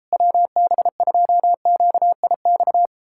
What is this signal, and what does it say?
Morse code, heard as a keyed tone, about 38 wpm, 700 Hz. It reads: WB2QIX